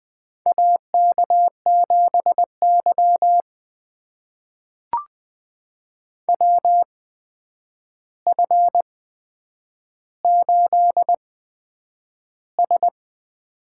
Morse code, heard as a keyed tone, about 20 wpm, spaced as for 9 wpm, 700 Hz. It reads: AK7Y WF8S
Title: USA Call Signs - 4 Characters - 8x Character Spacing - 20wpm